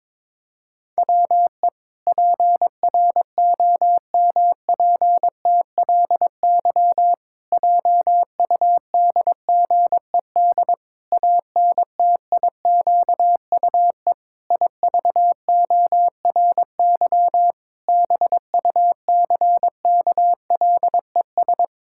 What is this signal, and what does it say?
Morse code, heard as a keyed tone, about 22 wpm, 700 Hz. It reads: WE PROMPTLY JUDGED ANTIQUE IVORY BUCKLES
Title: Pangram Challenge v2 - 22wpm